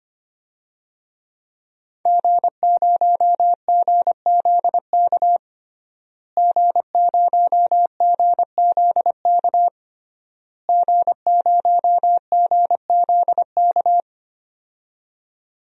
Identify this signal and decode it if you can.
Morse code, heard as a keyed tone, about 25 wpm, 700 Hz. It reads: G0GZK G0GZK G0GZK
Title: DXCC-100 Call Signs - 5 Characters - Repeated 3x - 25wpm